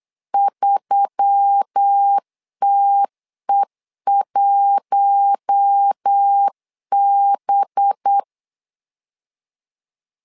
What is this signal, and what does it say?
Morse code, heard as a keyed tone, about 8 wpm, 800 Hz. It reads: 3TE1B
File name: LEZIONI / 5 LBVC / 5 LB 8wpm 2816